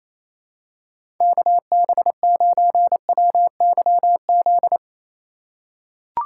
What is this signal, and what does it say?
Morse code, heard as a keyed tone, about 28 wpm, 700 Hz. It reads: KB9WYZ
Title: USA Call Signs - 6 Characters - 2x Character Spacing - 28wpm